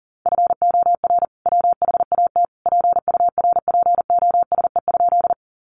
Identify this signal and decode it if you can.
Morse code, heard as a keyed tone, about 40 wpm, 700 Hz. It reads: FOR WHAT PURPOSE?
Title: Sentences from Top 800 Words 40wpm